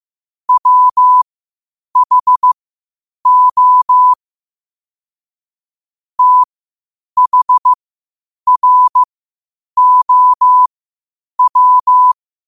Morse code, heard as a keyed tone, about 15 wpm, spaced as for 8 wpm, 1000 Hz. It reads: WHO THROW